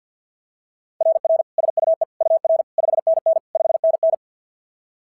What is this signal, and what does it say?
Morse code, heard as a keyed tone, about 47 wpm, 690 Hz. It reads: RR SUE UR 5NN 5NN
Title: SOTA QSO Practice 50wpm